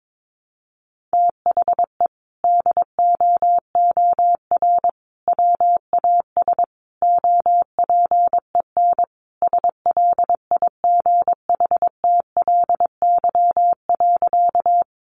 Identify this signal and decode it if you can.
Morse code, heard as a keyed tone, about 22 wpm, 700 Hz. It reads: THE DOOR WAS OPEN SLIGHTLY.
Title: Sentences from Top 1300 Words 22wpm